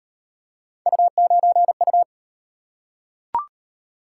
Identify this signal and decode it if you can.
Morse code, heard as a keyed tone, about 35 wpm, 700 Hz. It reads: U9U